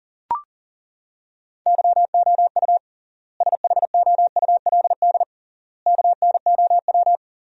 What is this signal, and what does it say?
Morse code, heard as a keyed tone, about 40 wpm, 700 Hz. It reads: YOU SHOULD KNOW
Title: 3x Word-Spacing - Sentences from Top 400 Words - 40wpm